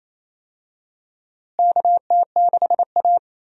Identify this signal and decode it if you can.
Morse code, heard as a keyed tone, about 28 wpm, 700 Hz. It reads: KT6A
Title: USA Call Signs - 4 Characters - Repeated 2x - 28wpm